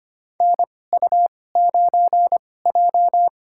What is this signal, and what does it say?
Morse code, heard as a keyed tone, about 25 wpm, spaced as for 18 wpm, 700 Hz. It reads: NU9J
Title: USA Call Signs - 4 Characters - 2x Character Spacing - 25wpm